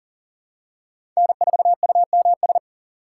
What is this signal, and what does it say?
Morse code, heard as a keyed tone, about 40 wpm, 700 Hz. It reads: N4UMS
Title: USA Call Signs - 5 Characters - 2x Character Spacing - 40wpm